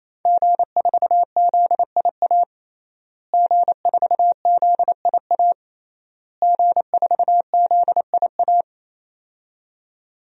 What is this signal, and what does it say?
Morse code, heard as a keyed tone, about 28 wpm, 700 Hz. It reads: G4ZIA G4ZIA G4ZIA
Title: DXCC-100 Call Signs - 5 Characters - Repeated 3x - 28wpm